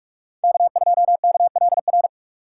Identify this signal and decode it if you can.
Morse code, heard as a keyed tone, about 43 wpm, 700 Hz. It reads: K2KLR
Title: USA Call Signs - 5 Characters - 2x Character Spacing - 45wpm